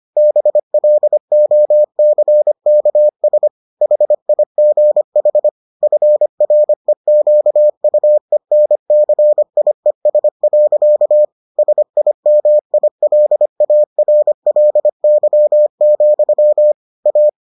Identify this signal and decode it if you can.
Morse code, heard as a keyed tone, about 25 wpm, 600 Hz. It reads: BLOCKS HIGH FREQUENCIES. SIMILARLY, A